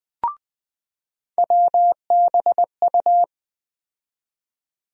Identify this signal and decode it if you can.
Morse code, heard as a keyed tone, about 20 wpm, 700 Hz. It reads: WBU
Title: USA Call Sign Postfixes - 3 Letters 20wpm